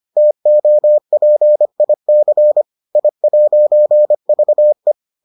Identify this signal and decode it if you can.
Morse code, heard as a keyed tone, about 25 wpm, 600 Hz. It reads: TOPIC I'VE